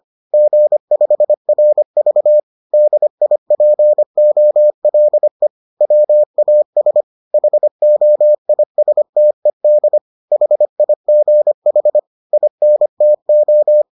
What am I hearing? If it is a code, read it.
Morse code, heard as a keyed tone, about 25 wpm, 600 Hz. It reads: G5RV DIPOLE WAS HOISTED HIGH INTO